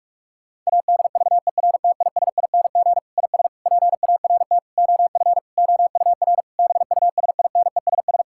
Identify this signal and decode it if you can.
Morse code, heard as a keyed tone, about 41 wpm, 700 Hz. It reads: ADVERTISING IS PART OF OUR BUSINESS
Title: Sentences from Top 1000 Words - Review Part 6 - 45wpm